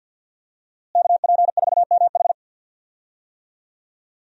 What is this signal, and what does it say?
Morse code, heard as a keyed tone, about 47 wpm, 700 Hz. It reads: KP4MH
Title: DXCC-100 Call Signs - 5 Characters - Repeated 2x - 50wpm